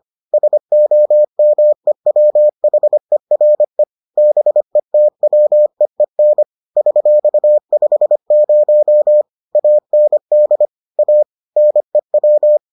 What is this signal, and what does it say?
Morse code, heard as a keyed tone, about 25 wpm, 600 Hz. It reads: SOMEWHERE BETWEEN $50 AND A NEW